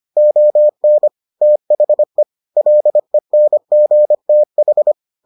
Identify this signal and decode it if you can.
Morse code, heard as a keyed tone, about 25 wpm, 600 Hz. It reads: ON THE LENGTH